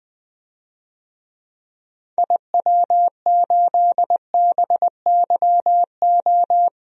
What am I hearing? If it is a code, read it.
Morse code, heard as a keyed tone, about 20 wpm, 700 Hz. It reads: IW8BYO
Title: DXCC-100 Call Signs - 6 Characters - 20wpm